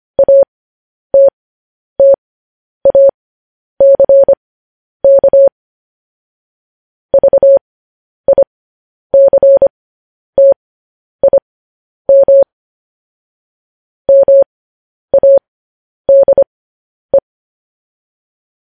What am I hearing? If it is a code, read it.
Morse code, heard as a keyed tone, about 25 wpm, spaced as for 10 wpm, 550 Hz. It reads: ATTACK VICTIM MADE
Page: news-headlines-in-morse-code-at-10-wpm